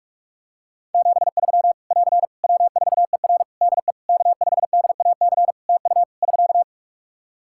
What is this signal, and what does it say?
Morse code, heard as a keyed tone, about 41 wpm, 700 Hz. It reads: 73 <AR> W4ER DE K5DAC TU <SK>
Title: K1USN SST QSO Practice 45wpm